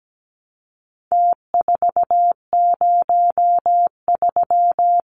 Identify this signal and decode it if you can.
Morse code, heard as a keyed tone, about 17 wpm, 700 Hz. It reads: T403
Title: Numbers - Contest Style - 4-Digit 17wpm